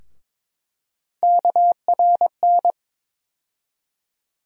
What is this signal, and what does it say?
Morse code, heard as a keyed tone, about 22 wpm, 700 Hz. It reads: KRN
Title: USA Call Sign Postfixes - 3 Letters 22wpm